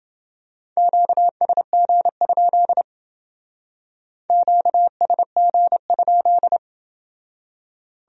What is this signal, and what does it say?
Morse code, heard as a keyed tone, about 29 wpm, 700 Hz. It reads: QSG? QSG?